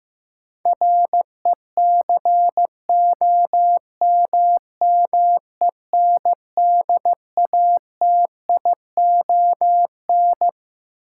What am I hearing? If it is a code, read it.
Morse code, heard as a keyed tone, about 15 wpm, 700 Hz. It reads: RECOMMENDATION
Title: Sentences from Top 1400 Words - New Word Drill - 15wpm